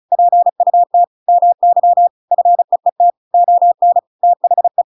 Morse code, heard as a keyed tone, about 34 wpm, 700 Hz. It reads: PUT MY FEET ON THE
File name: Pinocchio / Pinocchio ch7 35 35 ik1hge